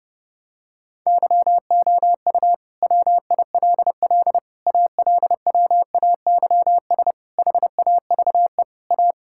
Morse code, heard as a keyed tone, about 30 wpm, 700 Hz. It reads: YOU WILL ALWAYS HAVE A
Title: Sentences from Top 1600 Words 30wpm